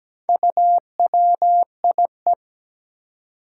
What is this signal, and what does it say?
Morse code, heard as a keyed tone, about 17 wpm, 700 Hz. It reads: UWIE
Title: Four Challenging Letter-Number Combinations 17wpm